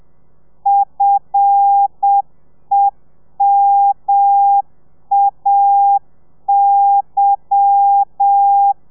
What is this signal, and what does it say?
Morse code, heard as a keyed tone, about 7 wpm, 800 Hz. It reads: FEMAY
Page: Lezione 1: audio a 30 CPM